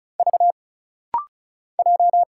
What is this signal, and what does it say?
Morse code, heard as a keyed tone, about 36 wpm, 700 Hz. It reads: V J
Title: Single Letter-Number Alt - Rapid-Fire 35wpm